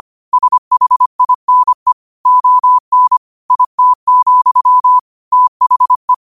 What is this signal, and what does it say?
Morse code, heard as a keyed tone, about 25 wpm, 1000 Hz. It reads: SHINE ON IT, THE
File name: qotdfiles / 25wpm